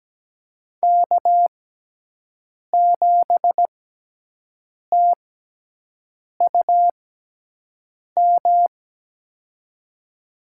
Morse code, heard as a keyed tone, about 17 wpm, spaced as for 6 wpm, 700 Hz. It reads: K7TUM
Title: USA Call Signs - 5 Characters - 6x Character Spacing - 17wpm